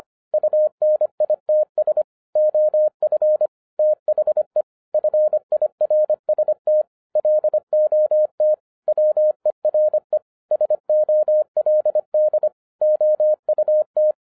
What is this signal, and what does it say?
Morse code, heard as a keyed tone, about 24 wpm, 610 Hz. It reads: UNITS OF THE FIRST LOT WERE SOLD OUT